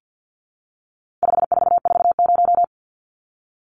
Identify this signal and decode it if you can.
Morse code, heard as a keyed tone, about 50 wpm, 700 Hz. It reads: 5440
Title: Numbers - 4-Digit 50wpm